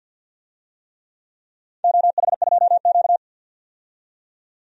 Morse code, heard as a keyed tone, about 47 wpm, 700 Hz. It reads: OH2Q
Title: DXCC-100 Call Signs - 4 Characters - 50wpm